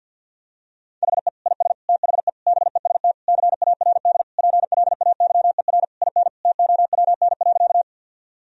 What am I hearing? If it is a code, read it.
Morse code, heard as a keyed tone, about 51 wpm, 700 Hz. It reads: HE IS THE BEST CARD PLAYER IN TOWN.
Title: Sentences from Top 1000 Words - Review Part 4 - 50wpm